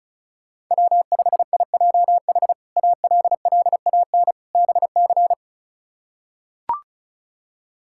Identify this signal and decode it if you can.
Morse code, heard as a keyed tone, about 34 wpm, 700 Hz. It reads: W5IJH ALLAN BC
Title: K1USN SST QSO Practice - 35wpm